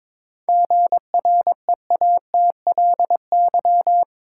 Morse code, heard as a keyed tone, about 22 wpm, 700 Hz. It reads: GREATLY
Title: Sentences from Top 1400 Words - New Word Drill - 22wpm